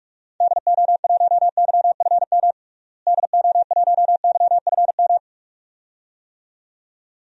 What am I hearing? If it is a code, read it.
Morse code, heard as a keyed tone, about 46 wpm, 700 Hz. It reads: DO1YFM DO1YFM
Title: DXCC-100 Call Signs - 6 Characters - Repeated 2x - 45wpm